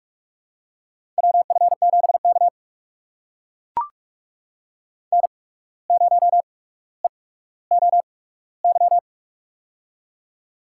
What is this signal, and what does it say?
Morse code, heard as a keyed tone, about 44 wpm, spaced as for 18 wpm, 700 Hz. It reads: WF7K N0EOY
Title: USA Call Signs - 8x Character Spacing - 45wpm